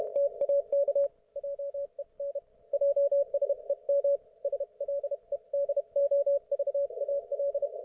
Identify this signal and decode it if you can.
Morse code, heard as a keyed tone, about 30 wpm, 560 Hz, with radio noise. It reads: TAK JEN JSEM SLEDOVAL